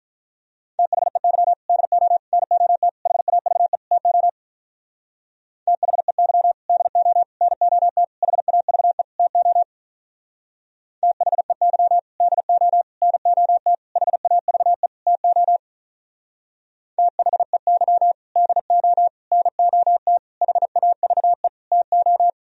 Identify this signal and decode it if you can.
Morse code, heard as a keyed tone, about 46 wpm, 700 Hz. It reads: THEY DO NOT HAVE TO THEY DO NOT HAVE TO THEY DO NOT HAVE TO THEY DO NOT HAVE TO